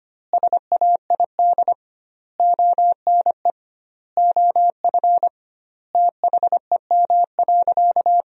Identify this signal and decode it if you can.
Morse code, heard as a keyed tone, about 25 wpm, 700 Hz. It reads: SAID ONE OF THEM.
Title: 2x Word-Spacing - Sentences from Top 300 Words - 25wpm